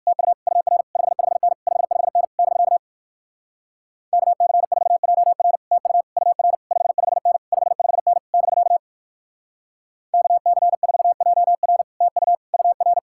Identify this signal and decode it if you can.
Morse code, heard as a keyed tone, about 59 wpm, 700 Hz. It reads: TU UR 55N 55N <BK> KC4JR TU UR 55N 55N <BK> KC4JR TU UR